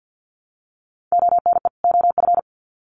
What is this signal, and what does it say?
Morse code, heard as a keyed tone, about 50 wpm, 700 Hz. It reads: ONE OF
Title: Sets of 2 Words 50wpm